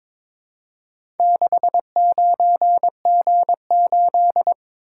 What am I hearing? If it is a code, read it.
Morse code, heard as a keyed tone, about 22 wpm, 700 Hz. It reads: 69G8
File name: practice / 22wpm / Four Letter-Number Combinations - 22wpm